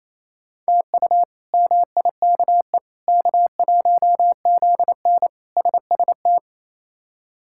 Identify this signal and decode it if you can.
Morse code, heard as a keyed tone, about 28 wpm, 700 Hz. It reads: TU MIKE K1ZN SST
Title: K1USN SST QSO Practice - 28wpm